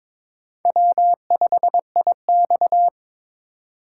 Morse code, heard as a keyed tone, about 26 wpm, 700 Hz. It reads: W5IX
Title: Speed Racing - USA 1x2 Call Signs 22wpm